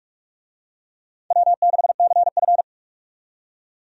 Morse code, heard as a keyed tone, about 46 wpm, 700 Hz. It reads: W6CF
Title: USA Call Signs - 4 Characters - 45wpm